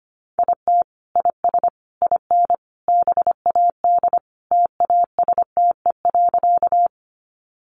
Morse code, heard as a keyed tone, about 25 wpm, 700 Hz. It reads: IT IS IN BAD TASTE.
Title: Sentences from Top 1100 Words 25wpm